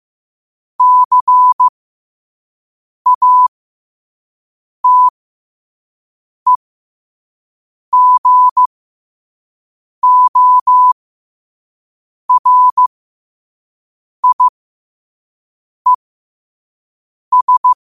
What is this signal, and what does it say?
Morse code, heard as a keyed tone, about 15 wpm, spaced as for 5 wpm, 1000 Hz. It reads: CATEGORIES